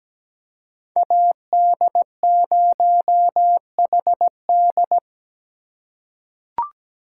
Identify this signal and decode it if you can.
Morse code, heard as a keyed tone, about 17 wpm, 700 Hz. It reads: AD0HD
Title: USA Call Signs - 5 Characters - 3x Character Spacing - 17wpm